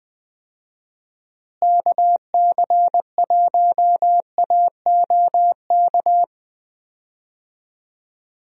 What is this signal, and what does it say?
Morse code, heard as a keyed tone, about 21 wpm, 700 Hz. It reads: KC1AOK
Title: USA Call Signs - 6 Characters - Repeated 2x - 20wpm